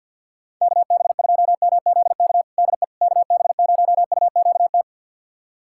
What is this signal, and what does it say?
Morse code, heard as a keyed tone, about 50 wpm, 700 Hz. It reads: KB2MZK DE KB0UQT
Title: USA Callsign DE Callsign 50wpm